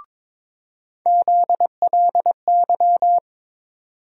Morse code, heard as a keyed tone, about 22 wpm, 700 Hz. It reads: ZLY
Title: USA Call Sign Postfixes - 3 Letters 22wpm